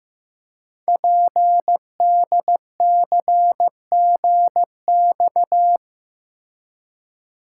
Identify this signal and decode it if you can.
Morse code, heard as a keyed tone, about 15 wpm, 700 Hz. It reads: PDCGX
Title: Code Groups - 5 Letters - 15wpm